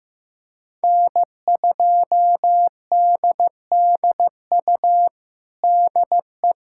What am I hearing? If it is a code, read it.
Morse code, heard as a keyed tone, about 15 wpm, 700 Hz. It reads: N2DDU DE
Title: USA Callsign DE Callsign - 5 Characters - 15wpm